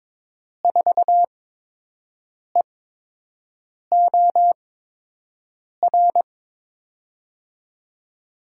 Morse code, heard as a keyed tone, about 22 wpm, spaced as for 6 wpm, 700 Hz. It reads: 4EOR
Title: USA Call Signs - 6 Characters - 8x Character Spacing - 22wpm